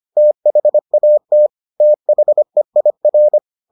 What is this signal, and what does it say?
Morse code, heard as a keyed tone, about 25 wpm, 600 Hz. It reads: THAT THEIR